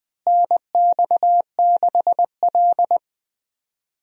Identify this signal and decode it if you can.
Morse code, heard as a keyed tone, about 20 wpm, 700 Hz. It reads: NX6L
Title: USA Call Signs - 4 Characters - Repeated 2x - 20wpm